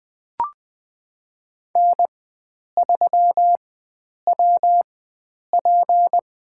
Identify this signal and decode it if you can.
Morse code, heard as a keyed tone, about 20 wpm, spaced as for 9 wpm, 700 Hz. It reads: N3WP
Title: USA Call Signs - 4x Character Spacing - 20wpm